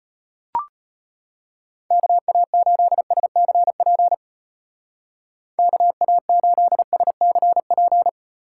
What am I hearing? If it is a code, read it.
Morse code, heard as a keyed tone, about 32 wpm, 700 Hz. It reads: KA8SCP KA8SCP